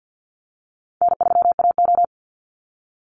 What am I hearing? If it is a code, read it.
Morse code, heard as a keyed tone, about 50 wpm, 700 Hz. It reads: N3AO
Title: Easy Call Signs - 50wpm